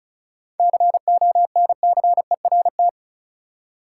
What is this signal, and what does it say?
Morse code, heard as a keyed tone, about 34 wpm, 700 Hz. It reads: CONCERT
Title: Seven Letter Words - 35wpm